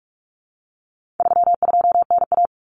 Morse code, heard as a keyed tone, about 44 wpm, 700 Hz. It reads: 32NA
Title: Common Cut Numbers - 4-Digit 45wpm